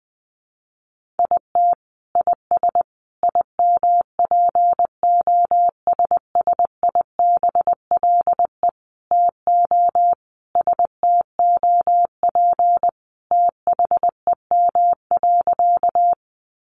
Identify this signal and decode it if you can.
Morse code, heard as a keyed tone, about 20 wpm, 700 Hz. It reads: IT IS IMPOSSIBLE TO STOP THEM.
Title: Sentences from Top 1100 Words 20wpm